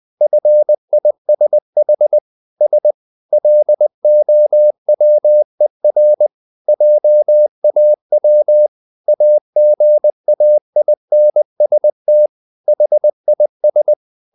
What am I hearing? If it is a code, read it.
Morse code, heard as a keyed tone, about 20 wpm, 600 Hz. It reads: FISH S LOWER JAW AGAINST HIS
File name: the old man and the sea part 2 20wpm